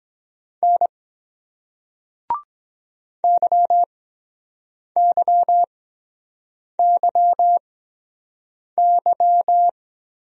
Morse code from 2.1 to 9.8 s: YYYY

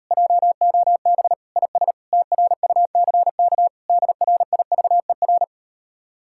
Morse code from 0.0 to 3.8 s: JOB IS TRUCK

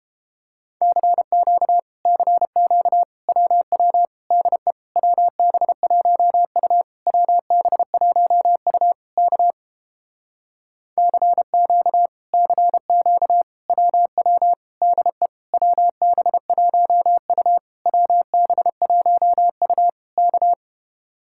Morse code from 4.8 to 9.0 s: WB1U WB1U

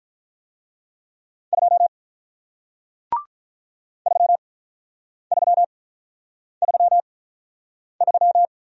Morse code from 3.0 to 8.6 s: 3333

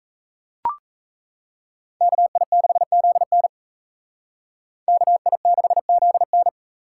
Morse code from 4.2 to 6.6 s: KI6ZN

